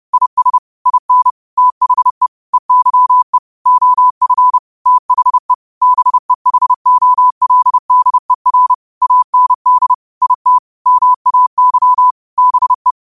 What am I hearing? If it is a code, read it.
Morse code, heard as a keyed tone, about 30 wpm, 1000 Hz. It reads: IS IN THE EYE OF THE BEHOLDER AND IT MAY BE
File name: qotdfiles / 30wpm